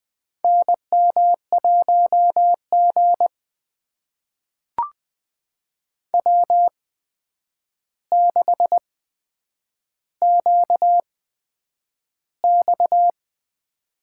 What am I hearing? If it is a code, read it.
Morse code, heard as a keyed tone, about 20 wpm, spaced as for 9 wpm, 700 Hz. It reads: NM1G W6QX
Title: USA Call Signs - 4 Characters - 8x Character Spacing - 20wpm